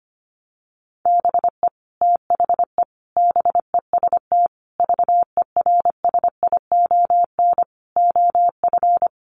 Morse code, heard as a keyed tone, about 25 wpm, 700 Hz. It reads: BE THE BEST VERSION OF
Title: Sentences from Top 1400 Words 25wpm